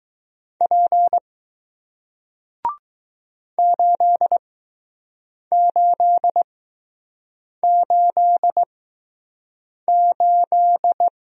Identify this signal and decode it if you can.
Morse code, heard as a keyed tone, about 20 wpm, spaced as for 6 wpm, 700 Hz. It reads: P 8888